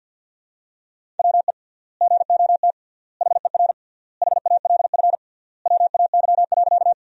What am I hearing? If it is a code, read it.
Morse code, heard as a keyed tone, about 50 wpm, 700 Hz. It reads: WE GOT HER HALF WAY.
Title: 3x Word-Spacing - Sentences from Top 400 Words - 50wpm